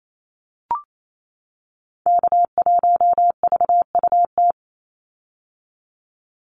Morse code, heard as a keyed tone, about 28 wpm, 700 Hz. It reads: K1VUT
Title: Call Signs - 28wpm